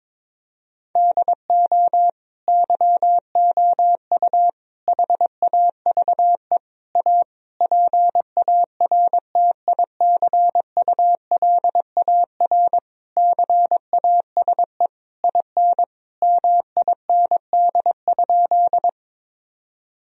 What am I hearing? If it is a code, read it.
Morse code, heard as a keyed tone, about 22 wpm, 700 Hz. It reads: DO YOU HAVE A PARTICULAR CASE IN MIND?
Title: Sentences from Top 1000 Words - Review Part 4 - 22wpm